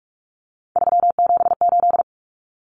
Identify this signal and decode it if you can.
Morse code, heard as a keyed tone, about 44 wpm, 700 Hz. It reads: 378
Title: Common Cut Numbers - 3-Digit 45wpm